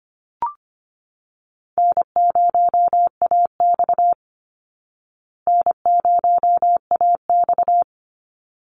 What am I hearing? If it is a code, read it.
Morse code, heard as a keyed tone, about 25 wpm, 700 Hz. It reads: N0AX N0AX
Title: Call Signs Repeated 3x - 25wpm